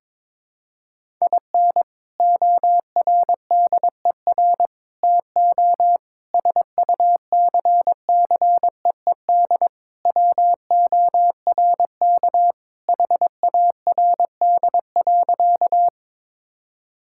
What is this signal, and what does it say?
Morse code, heard as a keyed tone, about 22 wpm, 700 Hz. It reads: IN ORDER TO SUCCEED WORK HARD.
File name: practice / 22wpm / Sentences with 3-Word Phrases - 22wpm